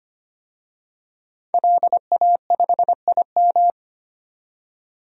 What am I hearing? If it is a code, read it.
Morse code, heard as a keyed tone, about 25 wpm, 700 Hz. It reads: LA5IM
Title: DXCC-100 Call Signs - 5 Characters - 25wpm